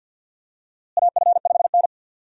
Morse code, heard as a keyed tone, about 52 wpm, 700 Hz. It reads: AF5N
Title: USA Call Signs - 4 Characters - Repeated 2x - 50wpm